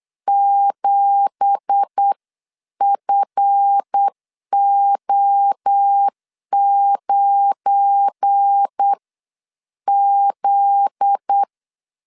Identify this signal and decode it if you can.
Morse code, heard as a keyed tone, about 8 wpm, spaced as for 7 wpm, 800 Hz. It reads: 7FO9Z
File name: CW / DET / 1 9 1500C S B 8wpm 5624